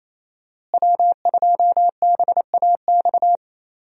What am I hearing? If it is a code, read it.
Morse code, heard as a keyed tone, about 28 wpm, 700 Hz. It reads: W2BAX
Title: USA Call Signs - 5 Characters - Repeated 2x - 28wpm